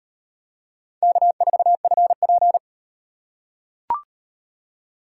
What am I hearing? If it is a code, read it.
Morse code, heard as a keyed tone, about 36 wpm, 700 Hz. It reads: K4FP